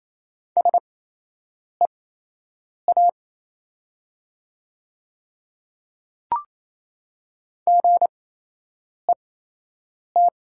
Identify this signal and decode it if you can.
Morse code, heard as a keyed tone, about 28 wpm, spaced as for 8 wpm, 700 Hz. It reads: SEA GET
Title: 8x Character Spacing - Top 300 Words - 28wpm